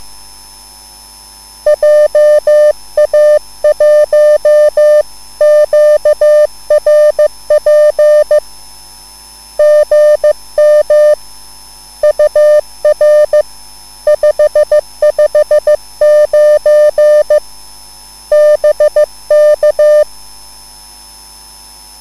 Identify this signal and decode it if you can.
Morse code, heard as a keyed tone, about 15 wpm, 600 Hz. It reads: JA1QRP GM UR 559 BK
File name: cwbook